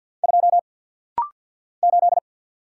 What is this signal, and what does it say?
Morse code, heard as a keyed tone, about 50 wpm, 700 Hz. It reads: 2 8
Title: Single Letter-Number - Rapid-Fire 50wpm